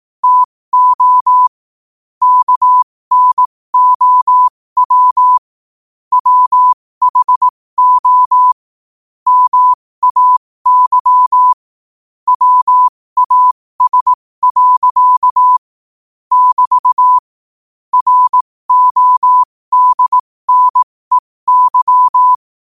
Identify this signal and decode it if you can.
Morse code, heard as a keyed tone, about 18 wpm, spaced as for 15 wpm, 1000 Hz. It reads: TO KNOW WHO MAY WAS. <BT> RODNEY
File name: qotdfiles / 15wpm